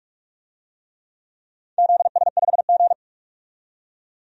Morse code, heard as a keyed tone, about 44 wpm, 700 Hz. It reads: ZS5G
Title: DXCC-100 Call Signs - 4 Characters - 45wpm